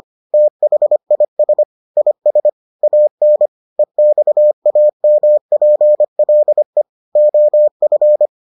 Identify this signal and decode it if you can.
Morse code, heard as a keyed tone, about 25 wpm, 600 Hz. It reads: THIS IS AN EXAMPLE OF